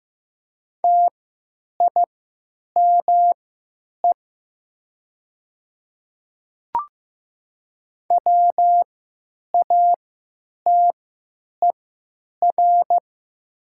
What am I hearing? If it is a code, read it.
Morse code, heard as a keyed tone, about 15 wpm, spaced as for 9 wpm, 700 Hz. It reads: TIME WATER
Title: 3x Character Spacing - Top 100 Words - 15wpm